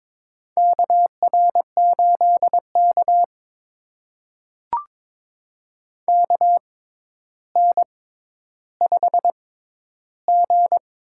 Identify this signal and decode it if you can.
Morse code, heard as a keyed tone, about 22 wpm, spaced as for 11 wpm, 700 Hz. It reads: KR8K KN5G
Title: USA Call Signs - 4 Characters - 6x Character Spacing - 22wpm